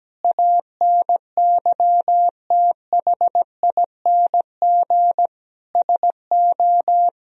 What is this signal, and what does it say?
Morse code, heard as a keyed tone, about 17 wpm, 700 Hz. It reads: ANYTHING SO
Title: Sentences from Top 1000 Words - Review Part 1 - 17wpm